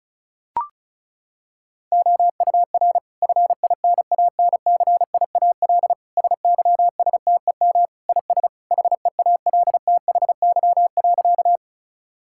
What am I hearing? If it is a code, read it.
Morse code, heard as a keyed tone, about 36 wpm, 700 Hz. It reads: OUR FINANCIAL SYSTEM IS HEALTHY.
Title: Sentences from Top 1000 Words - Review Part 5 - 35wpm